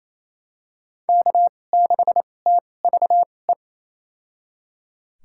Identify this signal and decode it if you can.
Morse code, heard as a keyed tone, about 28 wpm, spaced as for 20 wpm, 700 Hz. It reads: K6TVE
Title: USA Call Signs - 2x Character Spacing - 28wpm